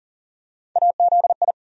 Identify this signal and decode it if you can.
Morse code, heard as a keyed tone, about 40 wpm, 700 Hz. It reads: AZI
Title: USA Call Sign Postfixes - 3 Letters 40wpm